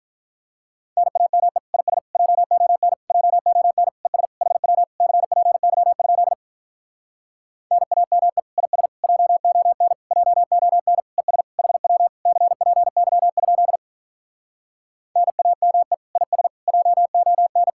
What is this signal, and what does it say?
Morse code, heard as a keyed tone, about 47 wpm, 700 Hz. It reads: NAME IS JON JON ES HW CPY? NAME IS JON JON ES HW CPY? NAME IS JON